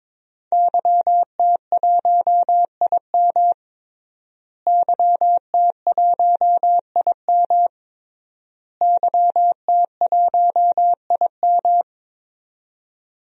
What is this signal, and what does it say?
Morse code, heard as a keyed tone, about 22 wpm, 700 Hz. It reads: YT1IM YT1IM YT1IM
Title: DXCC-100 Call Signs - 5 Characters - Repeated 3x - 22wpm